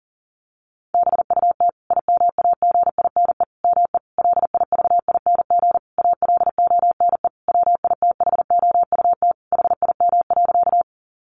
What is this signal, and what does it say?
Morse code, heard as a keyed tone, about 40 wpm, 700 Hz. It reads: BUT IMAGINE ME LIVING ALONE WITHOUT HIM.
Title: Sentences from Top 1400 Words 40wpm